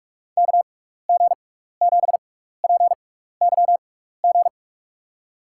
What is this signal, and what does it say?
Morse code, heard as a keyed tone, about 44 wpm, spaced as for 15 wpm, 700 Hz. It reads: KG7PYG
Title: USA Call Signs - 6x Character Spacing - 45wpm